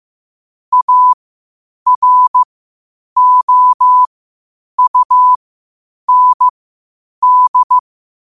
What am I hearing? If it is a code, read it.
Morse code, heard as a keyed tone, about 15 wpm, spaced as for 8 wpm, 1000 Hz. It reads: AROUND
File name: qotdfiles / 8wpm